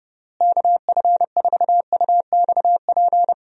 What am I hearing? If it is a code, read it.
Morse code, heard as a keyed tone, about 32 wpm, 700 Hz. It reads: KF4UXP
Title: USA Call Signs - 6 Characters - Repeated 3x - 30wpm